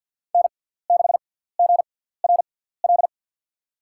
Morse code, heard as a keyed tone, about 50 wpm, spaced as for 17 wpm, 700 Hz. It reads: N6GRL
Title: USA Call Signs - 6x Character Spacing - 50wpm